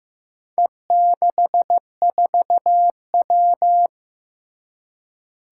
Morse code, heard as a keyed tone, about 15 wpm, 700 Hz. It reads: E64W
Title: Four Challenging Letter-Number Combinations 15wpm